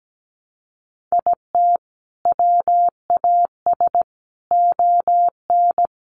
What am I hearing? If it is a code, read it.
Morse code, heard as a keyed tone, about 17 wpm, 700 Hz. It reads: IT WAS ON
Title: Sentences from Top 1200 Words 17wpm